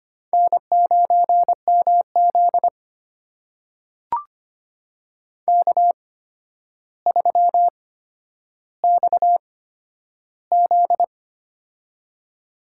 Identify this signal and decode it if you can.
Morse code, heard as a keyed tone, about 25 wpm, spaced as for 11 wpm, 700 Hz. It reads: N9MZ K3XZ
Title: USA Call Signs - 4 Characters - 8x Character Spacing - 25wpm